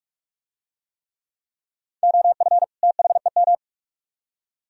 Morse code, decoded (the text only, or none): OF THEM